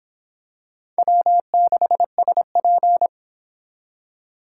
Morse code, heard as a keyed tone, about 25 wpm, 700 Hz. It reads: W6SP
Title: Speed Racing - USA 1x2 Call Signs 22wpm